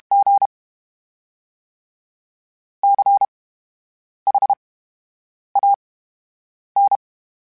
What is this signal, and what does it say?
Morse code, heard as a keyed tone, about 32 wpm, spaced as for 8 wpm, 800 Hz. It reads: G CHAN
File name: Special INGs / Special INGs-mp3s / VerySlowToFaster Ing @30